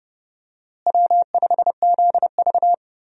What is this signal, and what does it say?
Morse code, heard as a keyed tone, about 30 wpm, 700 Hz. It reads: W5ZV
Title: USA Call Signs - 4 Characters - Repeated 2x - 30wpm